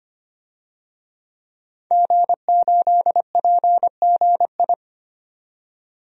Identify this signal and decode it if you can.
Morse code, heard as a keyed tone, about 25 wpm, 700 Hz. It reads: G8PGI
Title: DXCC-100 Call Signs - 5 Characters - 25wpm